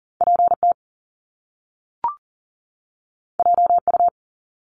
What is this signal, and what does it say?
Morse code, heard as a keyed tone, about 41 wpm, 700 Hz. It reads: PT JU
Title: Top Two Letter Combinations in Words 40wpm